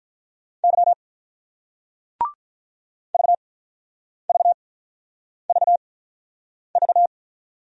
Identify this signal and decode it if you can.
Morse code, heard as a keyed tone, about 43 wpm, spaced as for 9 wpm, 700 Hz. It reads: Y VVVV
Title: Speed-Racing Single Letters 35wpm